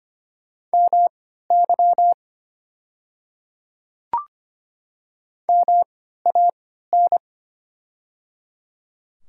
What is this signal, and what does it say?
Morse code, heard as a keyed tone, about 25 wpm, spaced as for 14 wpm, 690 Hz. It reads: MY MAN
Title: 3x Character Spacing - Top 500 Words - 25wpm